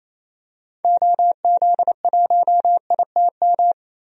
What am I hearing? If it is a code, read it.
Morse code, heard as a keyed tone, about 27 wpm, 700 Hz. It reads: OZ1ITM
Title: DXCC-100 Call Signs - 6 Characters - 28wpm